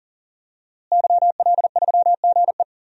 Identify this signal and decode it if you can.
Morse code, heard as a keyed tone, about 39 wpm, 700 Hz. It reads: YL3GE
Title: DXCC-100 Call Signs - 5 Characters - Repeated 3x - 40wpm